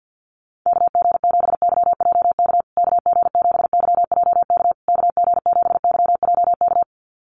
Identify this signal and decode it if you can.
Morse code, heard as a keyed tone, about 48 wpm, 700 Hz. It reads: KG7YJK KG7YJK KG7YJK
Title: USA Call Signs Repeated 3x 50wpm